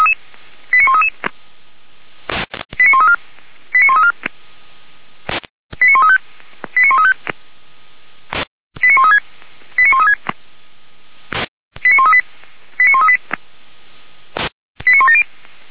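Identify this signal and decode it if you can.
Five-tone selective call (ZVEI), heard as quick runs of five tones, tones of about 70 ms. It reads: 89133 89134 89134 89135 89135 89136 89136 89138 89138 89170
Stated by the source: ZVEI five-tone call received by SDR, logged on the audio server of Feuerwehr Orlamünde.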